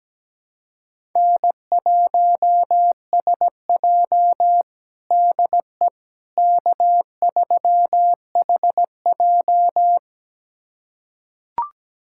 Morse code, heard as a keyed tone, about 17 wpm, 700 Hz. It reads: N1SJ DE K3HJ
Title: USA Callsign DE Callsign - 4 Characters - 17wpm